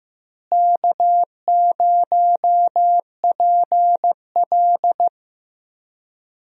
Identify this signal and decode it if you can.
Morse code, heard as a keyed tone, about 14 wpm, 700 Hz. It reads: K0PL
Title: USA Call Signs - 4 Characters - Repeated 3x - 15wpm